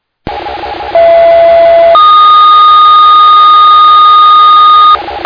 Two-tone page, heard as a one-second tone, then a three-second tone, about 670 then 1230 Hz. It is heard from 0.9 to 5.0 s.